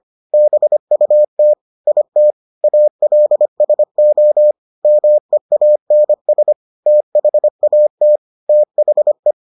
Morse code, heard as a keyed tone, about 25 wpm, 600 Hz. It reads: BUT IT ALSO MEANS THAT THE